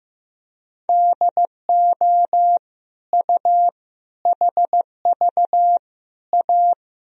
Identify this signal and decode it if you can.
Morse code, heard as a keyed tone, about 15 wpm, 700 Hz. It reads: DO U HV A
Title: Sentences from Top 300 Words and Abbreviations 15wpm